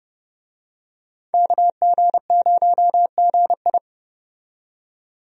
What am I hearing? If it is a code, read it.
Morse code, heard as a keyed tone, about 32 wpm, 700 Hz. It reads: KG0GI
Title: USA Call Signs - 5 Characters - 30wpm